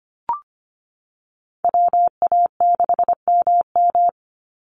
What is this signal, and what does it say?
Morse code, heard as a keyed tone, about 25 wpm, 700 Hz. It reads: WA6MM
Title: NA SOTA Chaser Callsigns 25wpm